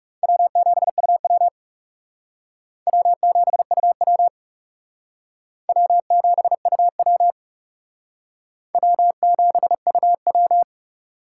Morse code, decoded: W7UW W7UW W7UW W7UW

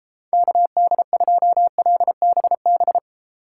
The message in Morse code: KD2LBB

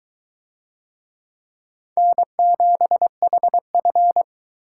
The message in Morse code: N7HF

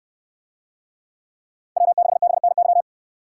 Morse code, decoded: W6BNY

